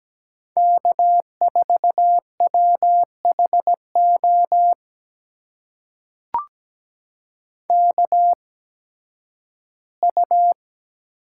Morse code, heard as a keyed tone, about 17 wpm, spaced as for 11 wpm, 700 Hz. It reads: K4WHO KU
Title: USA Call Signs - 5 Characters - 8x Character Spacing - 17wpm